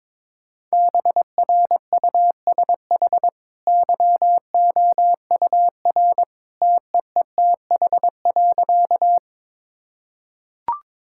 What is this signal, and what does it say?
Morse code, heard as a keyed tone, about 22 wpm, 690 Hz. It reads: BRUSH YOUR TEETH.
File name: practice / 22wpm / Easy Sentences with Hints - 22wpm